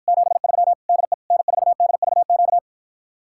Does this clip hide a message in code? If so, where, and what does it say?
Yes: Morse code from 0.1 to 2.6 s, heard as a keyed tone, about 54 wpm, 700 Hz. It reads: 73 DE N4DVQ